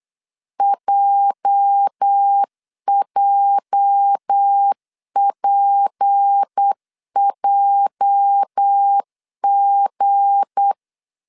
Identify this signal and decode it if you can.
Morse code, heard as a keyed tone, about 8 wpm, 800 Hz. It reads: JJPJG
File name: LEZIONI / 8 YJ45 / 8 YJ 8wpm 3313